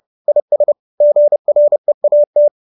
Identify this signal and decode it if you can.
Morse code, heard as a keyed tone, about 30 wpm, 590 Hz. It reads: IS GREAT